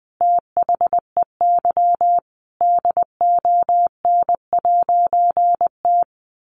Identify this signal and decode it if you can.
Morse code, heard as a keyed tone, about 20 wpm, 700 Hz. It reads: THEY DON'T
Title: A Sound Of Thunder 20wpm